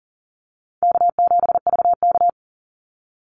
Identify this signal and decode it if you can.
Morse code, heard as a keyed tone, about 40 wpm, 700 Hz. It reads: K7VK
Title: NA SOTA Activator Callsigns 40wpm